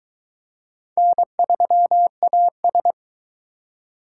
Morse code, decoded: N3AS

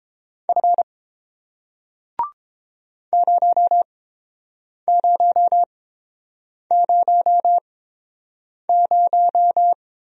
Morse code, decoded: F 0000

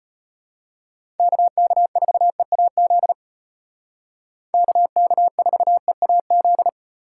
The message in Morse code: KK4EAZ KK4EAZ

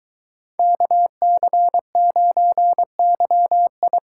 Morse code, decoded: KC9YI

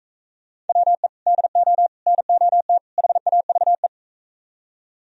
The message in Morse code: WE DO NOT HAVE

